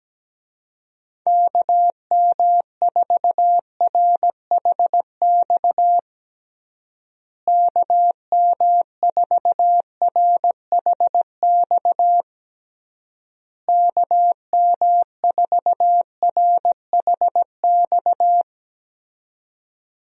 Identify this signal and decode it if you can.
Morse code, heard as a keyed tone, about 17 wpm, 700 Hz. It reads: KM4RHX KM4RHX KM4RHX
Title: USA Call Signs - 6 Characters - Repeated 3x - 17wpm